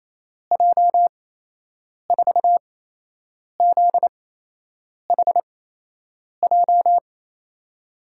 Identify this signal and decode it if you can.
Morse code, heard as a keyed tone, about 28 wpm, spaced as for 8 wpm, 700 Hz. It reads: J4ZHJ
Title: USA Call Signs - 6 Characters - 8x Character Spacing - 28wpm